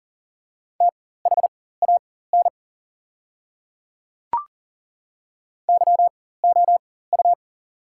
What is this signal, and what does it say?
Morse code, heard as a keyed tone, about 40 wpm, spaced as for 19 wpm, 690 Hz. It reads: THAN YOU
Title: 4x Character Spacing - Top 300 Words - 40wpm